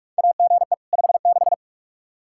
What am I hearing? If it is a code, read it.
Morse code, heard as a keyed tone, about 46 wpm, 700 Hz. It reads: AGE 56
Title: K1USN SST QSO Practice 45wpm